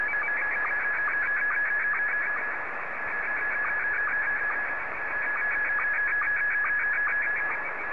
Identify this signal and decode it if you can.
SSTV (slow-scan television), Scottie 1.